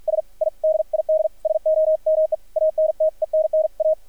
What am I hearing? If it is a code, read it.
Morse code, heard as a keyed tone, about 50 wpm, spaced as for 44 wpm, 630 Hz, with radio noise. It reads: S I DID SOME ANTENNA
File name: archived / CRSnet